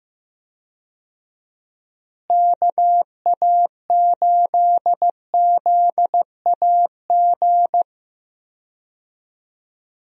Morse code, heard as a keyed tone, about 15 wpm, 700 Hz. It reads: KA8ZAG
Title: USA Call Signs - 6 Characters - Repeated 2x - 15wpm